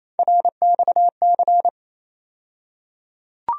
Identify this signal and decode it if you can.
Morse code, heard as a keyed tone, about 28 wpm, 700 Hz. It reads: RXC
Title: USA Call Sign Postfixes - 3 Letters 28wpm